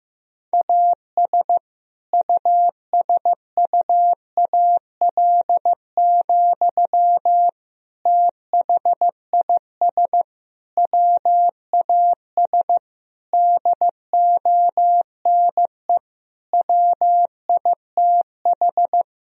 Morse code, heard as a keyed tone, about 15 wpm, 700 Hz. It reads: AS USUAL, THIS WAS DONE WITH
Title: Sentences from Top 1000 Words - Review Part 2 - 15wpm